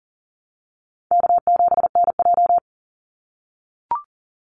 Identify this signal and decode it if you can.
Morse code, heard as a keyed tone, about 40 wpm, 700 Hz. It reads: K7NJ
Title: Easy Call Signs - 40wpm